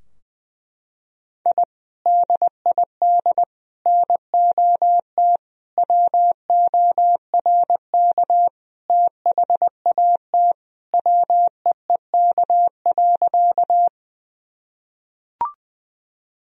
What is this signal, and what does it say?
Morse code, heard as a keyed tone, about 20 wpm, 700 Hz. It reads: I DID NOT WORK THAT WEEK.
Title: Sentences from Top 1000 Words - Review Part 2 - 20wpm